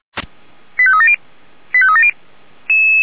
Five-tone selective call (ZVEI), heard as quick runs of five tones, tones of about 70 ms. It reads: 85370 85370
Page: feuerwehr-orlamuende